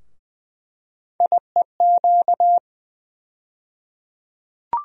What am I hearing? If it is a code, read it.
Morse code, heard as a keyed tone, about 20 wpm, 700 Hz. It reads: IEQ
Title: USA Call Sign Postfixes - 3 Letters 20wpm